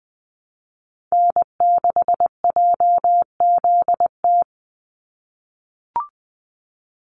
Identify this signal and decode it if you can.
Morse code, heard as a keyed tone, about 20 wpm, 700 Hz. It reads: N6JZT
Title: NA SOTA Activator Callsigns 20wpm